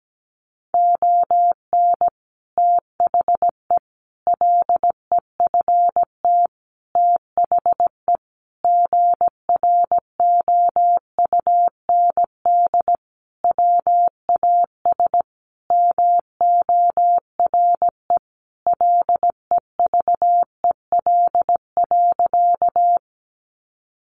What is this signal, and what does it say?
Morse code, heard as a keyed tone, about 17 wpm, 700 Hz. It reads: ON THE LEFT THE GROUND WAS MORE LEVEL.
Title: Sentences from Top 700 Words 17wpm